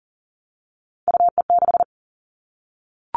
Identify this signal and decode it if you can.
Morse code, heard as a keyed tone, about 40 wpm, 700 Hz. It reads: UE6
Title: All Cut Numbers - 3-Digit 40wpm